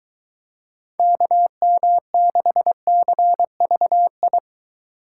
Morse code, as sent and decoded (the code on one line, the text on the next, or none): -.- -- -.... -.-. ...- ..
KM6CVI